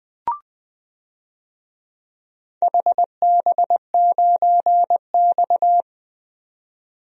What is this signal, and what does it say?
Morse code, heard as a keyed tone, about 20 wpm, 700 Hz. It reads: HB9X
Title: DXCC-100 Call Signs - 4 Characters - 20wpm